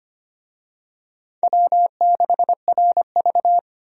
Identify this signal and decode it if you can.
Morse code, heard as a keyed tone, about 25 wpm, 700 Hz. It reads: W6RV
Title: USA Call Signs - 4 Characters - 25wpm